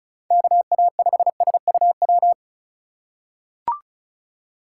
Morse code, heard as a keyed tone, about 34 wpm, 700 Hz. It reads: KA5SUW
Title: USA Call Signs - 6 Characters - 4x Character Spacing - 35wpm